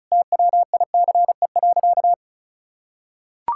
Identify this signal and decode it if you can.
Morse code, heard as a keyed tone, about 34 wpm, 700 Hz. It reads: TWICE.
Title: Sentences from Top 1600 Words 35wpm